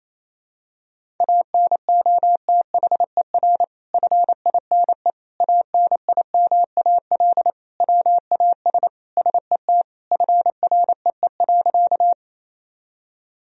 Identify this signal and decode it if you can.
Morse code, heard as a keyed tone, about 28 wpm, 700 Hz. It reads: ANOTHER FINE ANIMAL WAS SET FREE.
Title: Sentences from Top 1000 Words - Review Part 1 - 28wpm